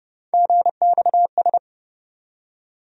Morse code, heard as a keyed tone, about 30 wpm, 700 Hz. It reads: GXS
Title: USA Call Sign Postfixes - 3 Letters 30wpm